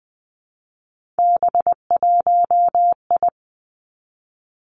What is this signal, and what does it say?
Morse code, heard as a keyed tone, about 20 wpm, 700 Hz. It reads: B1I
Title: Three Challenging Letter-Number Combinations 20wpm